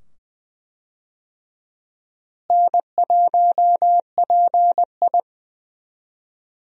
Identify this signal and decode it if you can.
Morse code, heard as a keyed tone, about 20 wpm, 700 Hz. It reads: N1PI
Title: USA Call Signs - 4 Characters - Repeated 2x - 20wpm